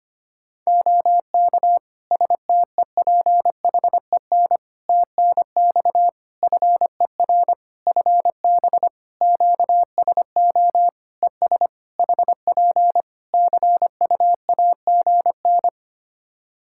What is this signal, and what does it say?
Morse code, heard as a keyed tone, about 25 wpm, 700 Hz. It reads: OK STEPHEN TNX FER FB QSO ES HP CUAGN
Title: Practice QSO - Exchange 4 - Ending - 25wpm